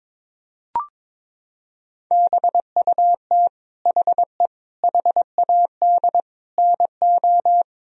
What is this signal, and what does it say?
Morse code, heard as a keyed tone, about 22 wpm, 700 Hz. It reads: BUT HE HAD NO
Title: Sentences from Top 1000 Words - Review Part 1 - 22wpm